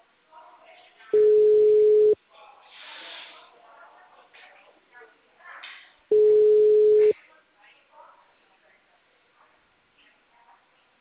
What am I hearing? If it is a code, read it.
Ringback tone (Europe, 425 Hz), with speech.